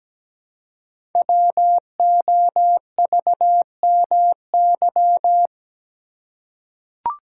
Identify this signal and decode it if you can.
Morse code, heard as a keyed tone, about 17 wpm, 700 Hz. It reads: WOVMY
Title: Code Groups - 5 Letters - 17wpm